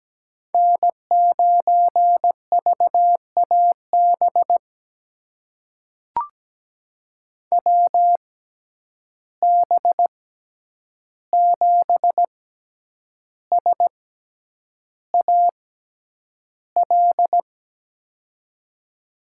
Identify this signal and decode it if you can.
Morse code, heard as a keyed tone, about 17 wpm, spaced as for 8 wpm, 700 Hz. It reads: N9VAB WB7SAL
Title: USA Call Signs - 6x Character Spacing - 17wpm